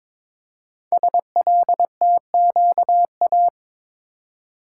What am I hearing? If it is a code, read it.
Morse code, heard as a keyed tone, about 22 wpm, 700 Hz. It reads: SLTQA
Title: Code Groups - 5 Letters - 22wpm